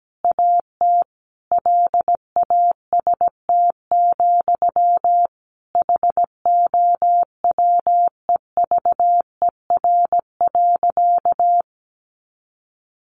Morse code, heard as a keyed tone, about 17 wpm, 700 Hz. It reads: AT LAST, HOWEVER.
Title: Sentences from Top 600 Words 17wpm